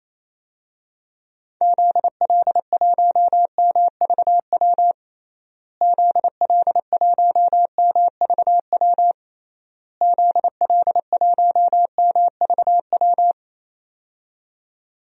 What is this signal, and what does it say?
Morse code, heard as a keyed tone, about 28 wpm, 700 Hz. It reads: ZL1MVW ZL1MVW ZL1MVW
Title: DXCC-100 Call Signs - 6 Characters - Repeated 3x - 28wpm